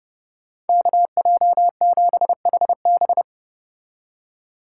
Morse code, heard as a keyed tone, about 31 wpm, 700 Hz. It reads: KJ7HB